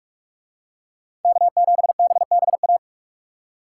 Morse code, heard as a keyed tone, about 41 wpm, 700 Hz. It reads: K7BBA